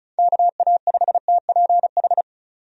Morse code, heard as a keyed tone, about 36 wpm, 700 Hz. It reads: KA5TPH
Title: USA Call Signs - 6 Characters - 35wpm